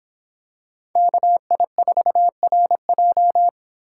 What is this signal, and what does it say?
Morse code, heard as a keyed tone, about 26 wpm, 700 Hz. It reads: KI4RJ